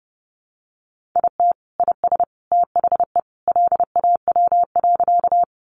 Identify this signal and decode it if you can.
Morse code, heard as a keyed tone, about 30 wpm, 700 Hz. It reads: IT IS THE LAW.
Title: Sentences from Top 600 Words 30wpm